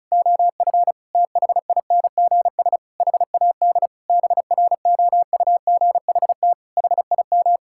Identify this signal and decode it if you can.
Morse code, heard as a keyed tone, about 34 wpm, 700 Hz. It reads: OF THINGS HAD BROUGHT HIM
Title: Sentences from Top 1000 Words - Review Part 6 - 35wpm